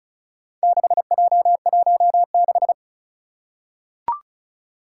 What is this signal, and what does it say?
Morse code, heard as a keyed tone, about 36 wpm, 690 Hz. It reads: 6J16
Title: Four Challenging Letter-Number Combinations 35wpm